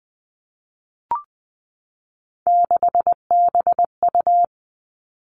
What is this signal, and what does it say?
Morse code, heard as a keyed tone, about 20 wpm, 700 Hz. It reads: 6BU
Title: Three Challenging Letter-Number Combinations 20wpm